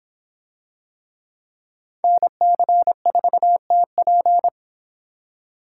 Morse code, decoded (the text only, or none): NC4TP